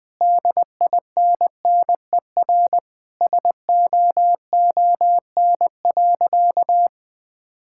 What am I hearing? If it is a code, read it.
Morse code, heard as a keyed tone, about 20 wpm, 700 Hz. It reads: DINNER SOON.
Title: Sentences with 3-Word Phrases - 20wpm